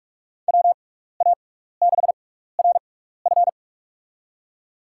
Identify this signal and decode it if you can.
Morse code, heard as a keyed tone, about 46 wpm, spaced as for 15 wpm, 700 Hz. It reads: WA6RF
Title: USA Call Signs - 5 Characters - 6x Character Spacing - 45wpm